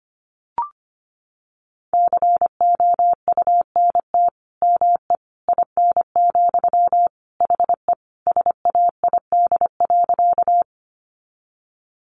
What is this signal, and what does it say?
Morse code, heard as a keyed tone, about 25 wpm, 700 Hz. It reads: COUNT ME IN, HE SAID.
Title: Sentences from Top 1500 Words 25wpm